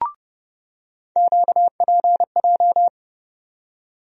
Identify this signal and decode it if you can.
Morse code, heard as a keyed tone, about 30 wpm, 700 Hz. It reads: QPJ
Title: USA Call Sign Postfixes - 3 Letters 30wpm